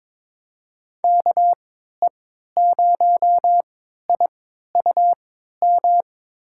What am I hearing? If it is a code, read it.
Morse code, heard as a keyed tone, about 22 wpm, spaced as for 12 wpm, 700 Hz. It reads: KE0IUM
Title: USA Call Signs - 6 Characters - 3x Character Spacing - 22wpm